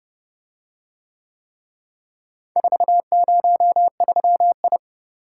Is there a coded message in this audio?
Yes: Morse code from 2.6 to 4.8 s, heard as a keyed tone, about 30 wpm, 700 Hz. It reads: 403I